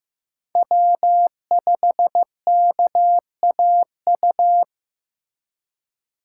Morse code, heard as a keyed tone, about 15 wpm, 700 Hz. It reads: W5KAU
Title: USA Call Signs - 2x Character Spacing - 15wpm